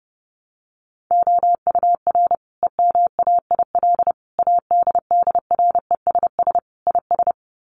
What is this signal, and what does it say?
Morse code, heard as a keyed tone, about 30 wpm, 700 Hz. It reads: OUR EMAIL ADDRESS IS
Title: Sets of 5 Words Encore2 30wpm